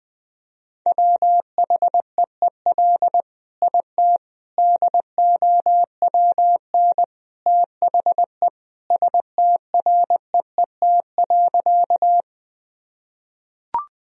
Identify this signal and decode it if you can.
Morse code, heard as a keyed tone, about 20 wpm, 700 Hz. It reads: WHEEL IT DOWN THE STREET.
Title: Sentences from Top 1000 Words - Review Part 3 - 20wpm